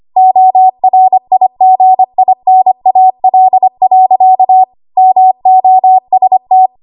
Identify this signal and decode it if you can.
Morse code, heard as a keyed tone, about 25 wpm, 750 Hz. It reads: ORIGINAL. MOST
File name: Archive / 25 WPM / 25WPM